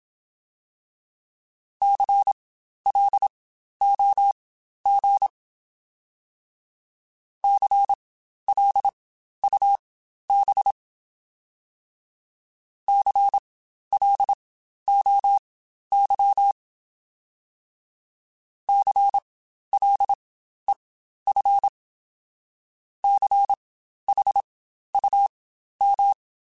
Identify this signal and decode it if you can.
Morse code, heard as a keyed tone, about 27 wpm, spaced as for 12 wpm, 800 Hz. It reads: CLOG CLUB CLOY CLEF CHUM